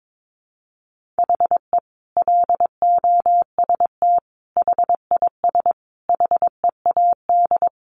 Morse code, heard as a keyed tone, about 22 wpm, 700 Hz. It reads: HE LOST HIS HEAD
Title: Sentences from Top 1400 Words 22wpm